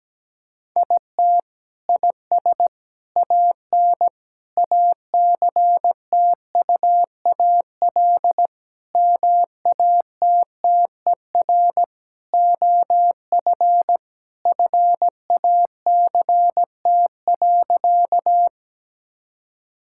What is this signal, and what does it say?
Morse code, heard as a keyed tone, about 17 wpm, 700 Hz. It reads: IT IS AN ACTUAL MATTER OF FACT.
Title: Sentences from Top 1000 Words - Review Part 5 - 17wpm